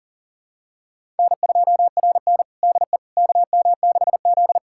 Morse code, heard as a keyed tone, about 40 wpm, 700 Hz. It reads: N2RN DE KM6Z
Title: USA Callsign DE Callsign - 4 Characters - 40wpm